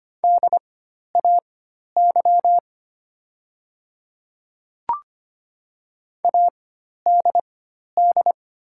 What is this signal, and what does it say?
Morse code, heard as a keyed tone, about 25 wpm, spaced as for 12 wpm, 690 Hz. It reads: DAY ADD
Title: 4x Character Spacing - Top 300 Words - 25wpm